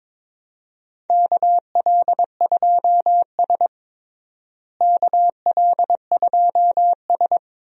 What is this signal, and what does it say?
Morse code, heard as a keyed tone, about 21 wpm, 700 Hz. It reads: KL2S KL2S
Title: USA Call Signs - 4 Characters - Repeated 2x - 22wpm